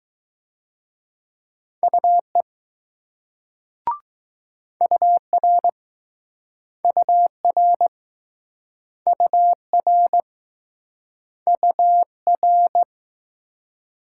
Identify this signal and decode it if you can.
Morse code, heard as a keyed tone, about 20 wpm, 700 Hz. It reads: UE UR UR UR UR